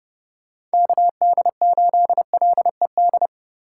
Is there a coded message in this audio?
Yes: Morse code from 0.7 to 3.3 s, heard as a keyed tone, about 31 wpm, 700 Hz. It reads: KD8LED